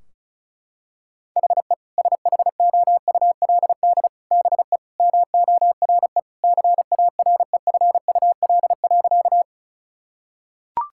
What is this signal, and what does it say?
Morse code, heard as a keyed tone, about 36 wpm, 700 Hz. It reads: HE SHOULD BE MORE CAREFUL.
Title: Sentences from Top 1600 Words 35wpm